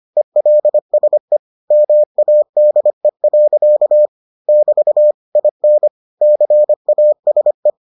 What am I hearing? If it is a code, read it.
Morse code, heard as a keyed tone, about 25 wpm, 600 Hz. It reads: ELSE MADE. <BT> IN CASE